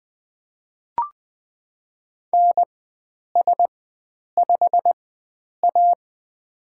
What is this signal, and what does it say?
Morse code, heard as a keyed tone, about 20 wpm, spaced as for 9 wpm, 700 Hz. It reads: NS5A
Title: USA Call Signs - 4 Characters - 4x Character Spacing - 20wpm